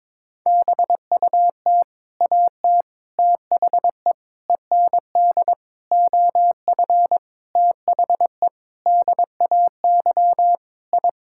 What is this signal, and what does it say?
Morse code, heard as a keyed tone, about 22 wpm, 700 Hz. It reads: BUT AT THE END OF THE DAY I
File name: practice / 22wpm / Sets of 8 Words - 22wpm